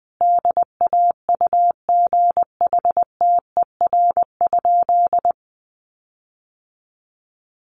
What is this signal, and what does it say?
Morse code, heard as a keyed tone, about 20 wpm, 700 Hz. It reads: DAUGHTER?
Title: Sentences from Top 1500 Words 20wpm